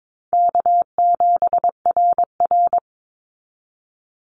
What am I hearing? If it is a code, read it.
Morse code, heard as a keyed tone, about 22 wpm, 700 Hz. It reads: K7RR
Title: Easy Call Signs Repeated 3x - 22wpm